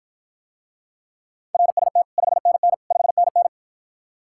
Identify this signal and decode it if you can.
Morse code, heard as a keyed tone, about 54 wpm, 700 Hz. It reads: RST 5NN 5NN